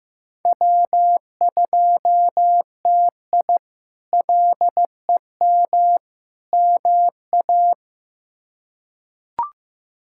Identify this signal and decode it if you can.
Morse code, heard as a keyed tone, about 15 wpm, 700 Hz. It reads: W2TI LEM MA
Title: K1USN SST QSO Practice - 15wpm